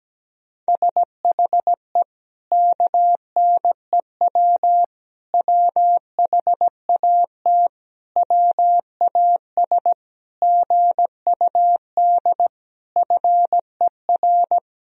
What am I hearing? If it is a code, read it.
Morse code, heard as a keyed tone, about 17 wpm, 700 Hz. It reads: SHE KNEW WHAT WAS GUD FER
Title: Sentences from Top 400 Words and Abbreviations 17wpm